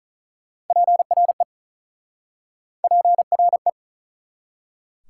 Morse code, heard as a keyed tone, about 48 wpm, 700 Hz. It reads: PRE PRE